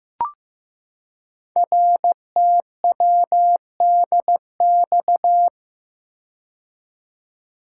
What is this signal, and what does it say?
Morse code, heard as a keyed tone, about 15 wpm, 700 Hz. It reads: RTWDX
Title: Code Groups - 5 Letters - 15wpm